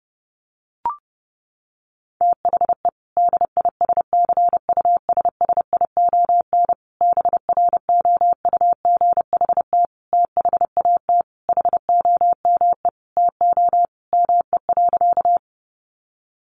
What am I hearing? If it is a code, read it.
Morse code, heard as a keyed tone, about 28 wpm, 700 Hz. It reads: THE DISCUSSION BROUGHT THAT HOME TO ME.
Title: Sentences from Top 1100 Words 30wpm